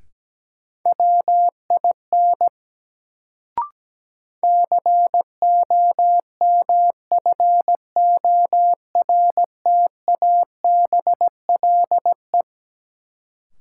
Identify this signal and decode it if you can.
Morse code, heard as a keyed tone, about 17 wpm, 700 Hz. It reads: WIN COMFORTABLE
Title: Sentences from Top 1500 Words - New Word Drill Alt - 17wpm